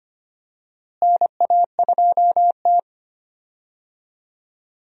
Morse code, heard as a keyed tone, about 25 wpm, 700 Hz. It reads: NA2T
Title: USA Call Signs - 4 Characters - Repeated 2x - 25wpm